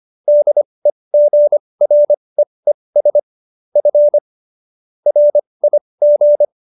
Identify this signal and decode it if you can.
Morse code, heard as a keyed tone, about 25 wpm, spaced as for 20 wpm, 590 Hz. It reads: DEGREES F RIG